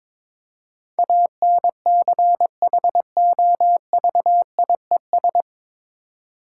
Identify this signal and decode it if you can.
Morse code, heard as a keyed tone, about 21 wpm, 690 Hz. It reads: ANCHOVIES
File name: Difficult Words with Hints - 22wpm